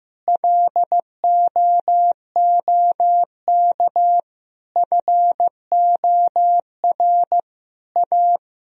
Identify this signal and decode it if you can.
Morse code, heard as a keyed tone, about 15 wpm, 700 Hz. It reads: LOOK FOR A
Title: Sentences from Top 1000 Words - Review Part 4 - 15wpm